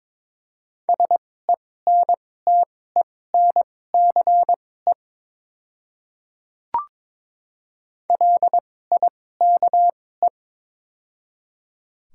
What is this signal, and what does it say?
Morse code, heard as a keyed tone, about 22 wpm, spaced as for 16 wpm, 700 Hz. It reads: SENTENCE LIKE
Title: 2x Character Spacing - Top 200 Words - 22wpm